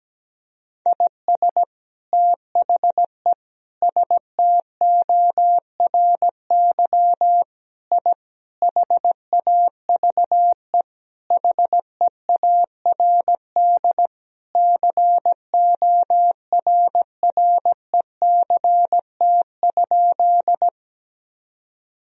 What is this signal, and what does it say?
Morse code, heard as a keyed tone, about 17 wpm, 700 Hz. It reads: IS THE STORY I HAVE HEARD CORRECT?
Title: Sentences from Top 1000 Words - Review Part 2 - 17wpm